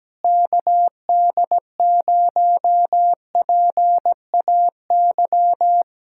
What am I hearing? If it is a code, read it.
Morse code, heard as a keyed tone, about 17 wpm, 700 Hz. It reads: KD0PAY
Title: Speed-Racing Common QSO Phrases Encore - 17wpm